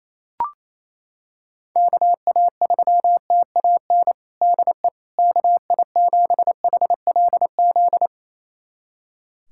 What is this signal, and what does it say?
Morse code, heard as a keyed tone, about 28 wpm, 700 Hz. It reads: KA3TAN DE KI7HLZ
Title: USA Callsign DE Callsign - 6 Characters - 28wpm